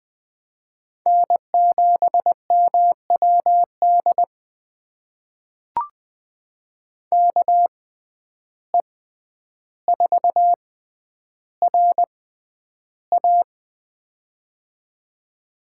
Morse code, heard as a keyed tone, about 20 wpm, spaced as for 10 wpm, 700 Hz. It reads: N7MWD KE4RA
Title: USA Call Signs - 5 Characters - 6x Character Spacing - 20wpm